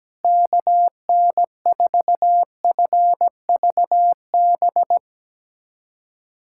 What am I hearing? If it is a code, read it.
Morse code, heard as a keyed tone, about 17 wpm, 700 Hz. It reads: KN4FVB
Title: USA Callsign DE Callsign - 6 Characters - 17wpm